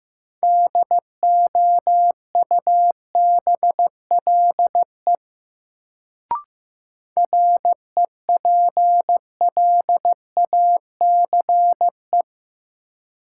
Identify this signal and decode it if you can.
Morse code, heard as a keyed tone, about 15 wpm, 700 Hz. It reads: DOUBLE REPLACE
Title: Sentences from Top 1400 Words - New Word Drill - 15wpm